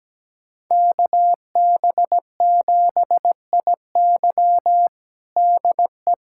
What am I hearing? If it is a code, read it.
Morse code, heard as a keyed tone, about 17 wpm, 700 Hz. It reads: KB7IY DE
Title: USA Callsign DE Callsign - 5 Characters - 17wpm